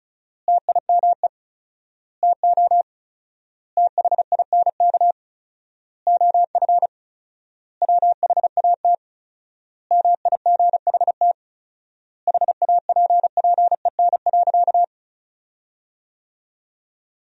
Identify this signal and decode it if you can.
Morse code, heard as a keyed tone, about 36 wpm, 700 Hz. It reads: TIME TO THINK OF WHAT MIGHT HAPPEN.
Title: 4x Word-Spacing - Sentences from Top 400 Words - 35wpm